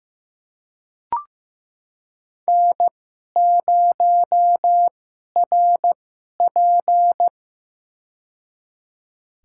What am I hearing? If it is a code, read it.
Morse code, heard as a keyed tone, about 15 wpm, spaced as for 11 wpm, 700 Hz. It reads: N0RP
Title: USA Call Signs - 4 Characters - 2x Character Spacing - 15wpm